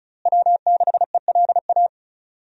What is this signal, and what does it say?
Morse code, heard as a keyed tone, about 36 wpm, 700 Hz. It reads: W6ELA
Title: USA Call Signs - 5 Characters - Repeated 3x - 35wpm